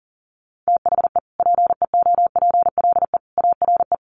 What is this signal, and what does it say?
Morse code, heard as a keyed tone, about 40 wpm, 700 Hz. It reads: THE PEOPLE ARE
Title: Sentences from Top 1400 Words 40wpm